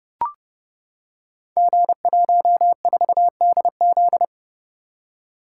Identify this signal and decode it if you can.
Morse code, heard as a keyed tone, about 31 wpm, 700 Hz. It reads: G14DZ
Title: Code Groups - 5 Letter:Numbers - 30wpm